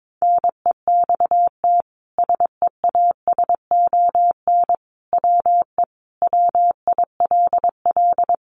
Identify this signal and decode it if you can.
Morse code, heard as a keyed tone, about 22 wpm, 700 Hz. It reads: NEXT SEASON WE WILL
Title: Sentences from Top 1400 Words 22wpm